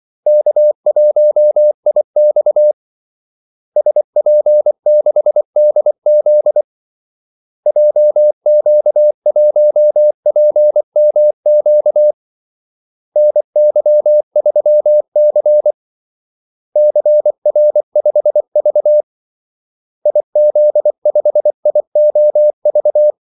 Morse code, heard as a keyed tone, about 24 wpm, 600 Hz. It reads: K1IX SP6DZ JQ1PMQ NY3C CR5V IZ5IOV
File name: Callsigns-24wpm